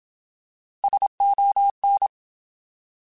Morse code, heard as a keyed tone, about 27 wpm, 800 Hz. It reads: SON
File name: Alphabetical / Alphabetical mp3s / Ss @25